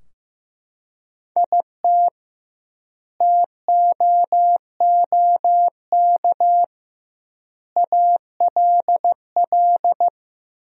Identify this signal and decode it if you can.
Morse code, heard as a keyed tone, about 15 wpm, 700 Hz. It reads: IT TOOK ALL
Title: 2x Word-Spacing - Sentences from Top 300 Words - 15wpm